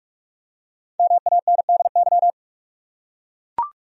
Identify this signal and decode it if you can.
Morse code, heard as a keyed tone, about 43 wpm, 700 Hz. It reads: MANDY
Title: USA Common Names - Max 5 Letters - 45wpm